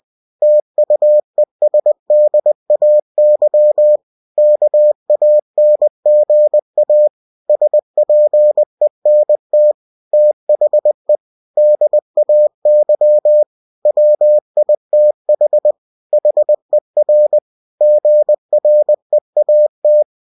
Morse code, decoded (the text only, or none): TUESDAY KANGA SPENT THE DAY WITH HER GREAT